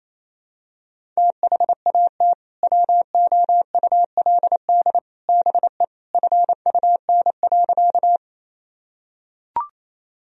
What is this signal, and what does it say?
Morse code, heard as a keyed tone, about 28 wpm, 700 Hz. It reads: THAT WOULD BE FUN.
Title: Sentences from Top 1000 Words - Review Part 4 - 28wpm